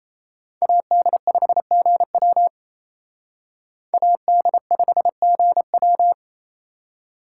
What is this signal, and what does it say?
Morse code, heard as a keyed tone, about 34 wpm, 700 Hz. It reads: AD5GW AD5GW
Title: Speed Racing - USA 2x2 Call Signs 28wpm